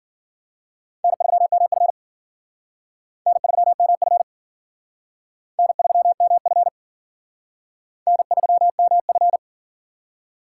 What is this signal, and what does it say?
Morse code, heard as a keyed tone, about 51 wpm, 700 Hz. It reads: N3MF N3MF N3MF N3MF